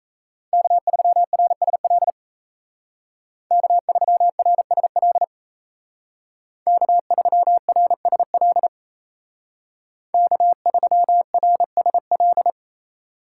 Morse code, heard as a keyed tone, about 33 wpm, 700 Hz. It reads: K3RSL K3RSL K3RSL K3RSL
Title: Speed Racing - USA 1x3 Call Signs 28wpm